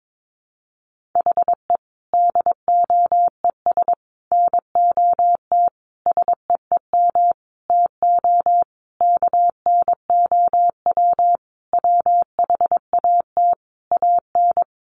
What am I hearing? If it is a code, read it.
Morse code, heard as a keyed tone, about 22 wpm, 700 Hz. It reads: HE DOES NOT SEEM TO KNOW WHAT AN
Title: Sentences from Top 1400 Words 22wpm